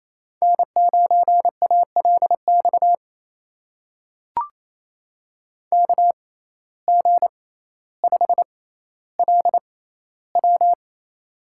Morse code, heard as a keyed tone, about 28 wpm, spaced as for 14 wpm, 700 Hz. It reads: N9ALX KG5LW
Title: USA Call Signs - 5 Characters - 6x Character Spacing - 28wpm